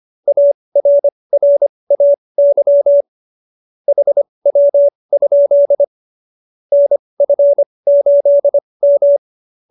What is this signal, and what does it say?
Morse code, heard as a keyed tone, about 25 wpm, spaced as for 20 wpm, 570 Hz. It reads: ARRAY HW? NF8M